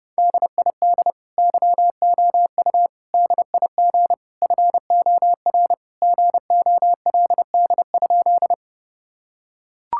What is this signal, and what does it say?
Morse code, heard as a keyed tone, about 30 wpm, 700 Hz. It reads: DID YOU DIG FOR GOLD?
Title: Sentences from Top 1600 Words 30wpm